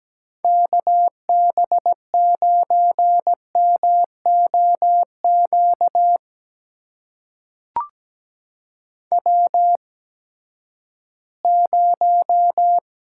Morse code, heard as a keyed tone, about 17 wpm, spaced as for 12 wpm, 700 Hz. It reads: KB9MOQ W0
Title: USA Call Signs - 8x Character Spacing - 17wpm